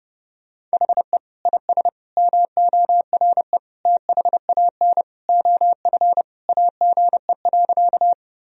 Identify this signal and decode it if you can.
Morse code, heard as a keyed tone, about 32 wpm, 700 Hz. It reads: HE IS MORE THAN OF AGE.
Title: Sentences from Top 1000 Words - Review Part 3 - 30wpm